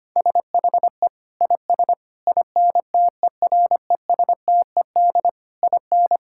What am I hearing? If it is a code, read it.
Morse code, heard as a keyed tone, about 25 wpm, 700 Hz. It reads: SHE IS INTERESTED IN
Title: Sentences from Top 1000 Words - Review Part 6 - 25wpm